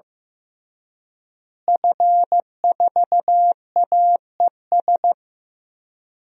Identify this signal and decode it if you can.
Morse code, heard as a keyed tone, about 15 wpm, 700 Hz. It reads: F4AES
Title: DXCC-100 Call Signs - 5 Characters - Repeated 2x - 15wpm